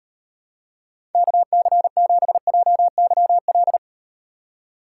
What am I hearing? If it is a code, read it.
Morse code, heard as a keyed tone, about 36 wpm, 700 Hz. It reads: KC7JYL